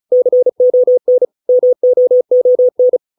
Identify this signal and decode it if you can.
Morse code, heard as a keyed tone, about 34 wpm, 500 Hz. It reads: CON MOON